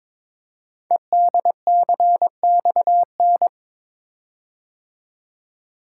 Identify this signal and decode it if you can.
Morse code, heard as a keyed tone, about 22 wpm, 700 Hz. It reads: EDCXN
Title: Code Groups - 5 Letters - 22wpm